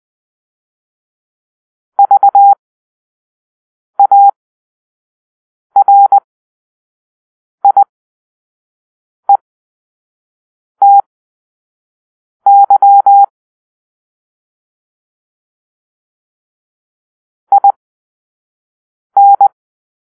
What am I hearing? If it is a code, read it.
Morse code, heard as a keyed tone, about 20 wpm, spaced as for 5 wpm, 800 Hz. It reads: VARIETY IN
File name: arrl / Grp-5 Drill-3